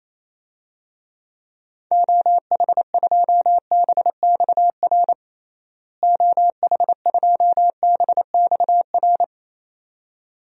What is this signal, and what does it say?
Morse code, heard as a keyed tone, about 27 wpm, 700 Hz. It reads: OH2BXR OH2BXR
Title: DXCC-100 Call Signs - 6 Characters - Repeated 2x - 28wpm